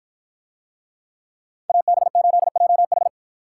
Morse code, heard as a keyed tone, about 51 wpm, 700 Hz. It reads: AB8JH